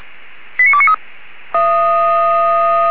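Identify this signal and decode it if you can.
Five-tone selective call (ZVEI), heard as quick runs of five tones, tones of about 70 ms. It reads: 89293
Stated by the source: ZVEI five-tone selective call 89288 received with an SDR, from the Feuerwehr Orlamünde fire brigade audio server.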